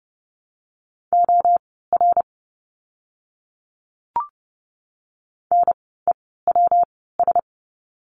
Morse code, heard as a keyed tone, about 29 wpm, spaced as for 17 wpm, 700 Hz. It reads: OR NEWS
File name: course / 30:10wpm / 115 Lesson 19f - Letter w - Words 30wpm